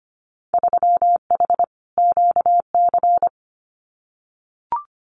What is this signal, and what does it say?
Morse code, heard as a keyed tone, about 25 wpm, 700 Hz. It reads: 3H QC
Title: Winter Field Day Exchanges 25wpm